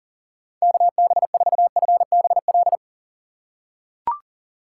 Morse code, heard as a keyed tone, about 40 wpm, 700 Hz. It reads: KB4FBL
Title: USA Call Signs - 6 Characters - 4x Character Spacing - 40wpm